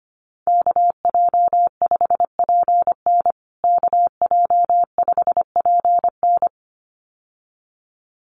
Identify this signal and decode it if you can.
Morse code, heard as a keyed tone, about 25 wpm, 700 Hz. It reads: KJ5PN KJ5PN
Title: USA Call Signs Repeated 2x 25wpm